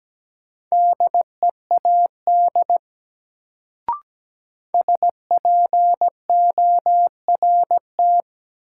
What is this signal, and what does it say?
Morse code, heard as a keyed tone, about 17 wpm, 700 Hz. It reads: DEAD SPORT
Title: Sentences from Top 1500 Words - New Word Drill Alt - 17wpm